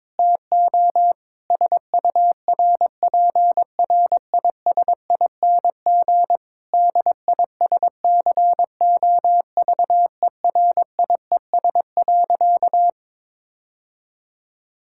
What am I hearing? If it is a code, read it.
Morse code, heard as a keyed tone, about 22 wpm, 700 Hz. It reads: TO SURPRISING DISCOVERIES.
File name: Difficult Sentences with Hints - 22wpm